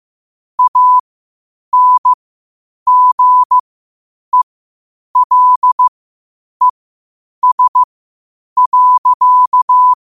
Morse code, heard as a keyed tone, about 15 wpm, spaced as for 8 wpm, 1000 Hz. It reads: ANGELES.
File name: qotdfiles / 8wpm